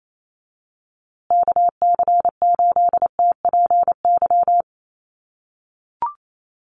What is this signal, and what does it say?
Morse code, heard as a keyed tone, about 28 wpm, 700 Hz. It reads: KC8TPY
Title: USA Call Signs Repeated 3x 28wpm